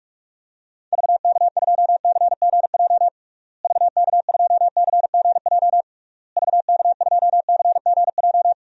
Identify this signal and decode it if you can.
Morse code, heard as a keyed tone, about 41 wpm, 700 Hz. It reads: VK2CGJ VK2CGJ VK2CGJ
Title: DXCC-100 Call Signs - 6 Characters - Repeated 3x - 45wpm